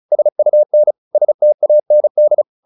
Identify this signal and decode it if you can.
Morse code, heard as a keyed tone, about 34 wpm, 600 Hz. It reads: SUN STAND